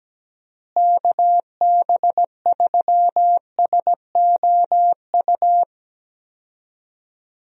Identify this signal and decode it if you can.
Morse code, heard as a keyed tone, about 17 wpm, 700 Hz. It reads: KB3SOU
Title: USA Call Signs - 6 Characters - Repeated 3x - 17wpm